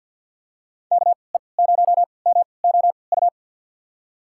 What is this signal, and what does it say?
Morse code, heard as a keyed tone, about 50 wpm, spaced as for 28 wpm, 700 Hz. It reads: KE0MOU
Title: USA Call Signs - 6 Characters - 3x Character Spacing - 50wpm